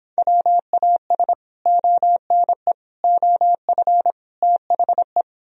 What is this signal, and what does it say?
Morse code, heard as a keyed tone, about 26 wpm, 700 Hz. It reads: WAS ONE OF THE